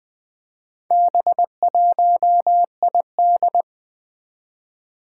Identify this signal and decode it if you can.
Morse code, heard as a keyed tone, about 21 wpm, 700 Hz. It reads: B1ID
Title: Four Challenging Letter-Number Combinations 20wpm